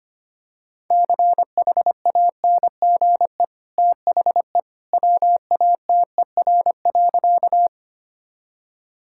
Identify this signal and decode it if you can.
Morse code, heard as a keyed tone, about 25 wpm, 700 Hz. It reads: CHANGE THE WATER.
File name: Easy Sentences with Hints - 25wpm